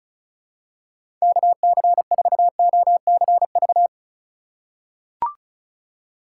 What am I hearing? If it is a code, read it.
Morse code, heard as a keyed tone, about 36 wpm, 700 Hz. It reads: KC4OCV
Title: USA Call Signs - 6 Characters - 2x Character Spacing - 35wpm